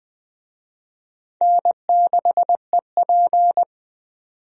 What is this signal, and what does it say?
Morse code, heard as a keyed tone, about 20 wpm, 700 Hz. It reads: N6EP
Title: USA Call Signs - 4 Characters - 20wpm